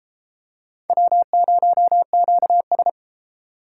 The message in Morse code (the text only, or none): W0QS